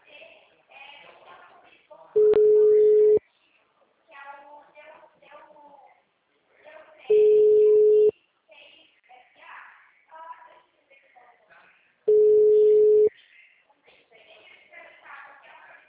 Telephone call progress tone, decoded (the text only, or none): ringback tone